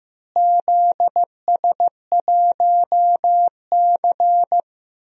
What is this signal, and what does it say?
Morse code, heard as a keyed tone, about 16 wpm, 700 Hz. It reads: ZS1C
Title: DXCC-100 Call Signs - 4 Characters - Repeated 3x - 15wpm